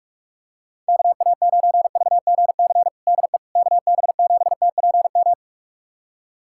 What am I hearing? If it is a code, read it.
Morse code, heard as a keyed tone, about 46 wpm, 700 Hz. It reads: KA9VGC DE KB7TPM
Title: USA Callsign DE Callsign - 6 Characters - 45wpm